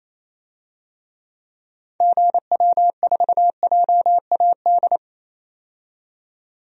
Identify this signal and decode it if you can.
Morse code, heard as a keyed tone, about 28 wpm, 700 Hz. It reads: GW4JAD
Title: DXCC-100 Call Signs - 6 Characters - 28wpm